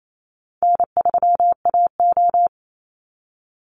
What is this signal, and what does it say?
Morse code, heard as a keyed tone, about 28 wpm, 700 Hz. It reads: N3AO
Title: Easy Call Signs - 28wpm